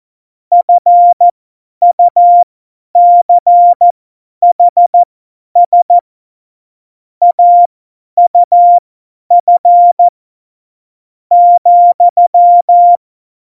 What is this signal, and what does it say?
Morse code, heard as a keyed tone, about 14 wpm, spaced as for 10 wpm, 700 Hz. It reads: FUCHS AUF ,
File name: Morsekurs / Witze / Witz 03. November 2020, 50 F70